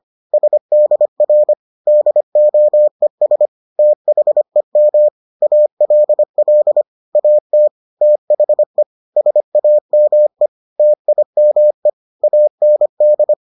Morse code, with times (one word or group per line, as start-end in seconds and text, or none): 0.3-1.5 SDR
1.9-3.5 DOES
3.8-5.1 THEM
5.4-6.8 ALL
7.2-7.7 AT
8.0-8.8 THE
9.2-10.5 SAME
10.8-11.9 TIME
12.2-13.3 AND